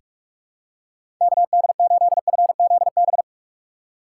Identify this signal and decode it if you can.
Morse code, heard as a keyed tone, about 46 wpm, 700 Hz. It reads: KD8FZB
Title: USA Call Signs - 6 Characters - 3x Character Spacing - 45wpm